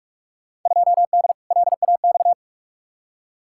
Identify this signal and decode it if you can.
Morse code, heard as a keyed tone, about 46 wpm, 700 Hz. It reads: 2D LAX
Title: Field Day Exchanges - 45wpm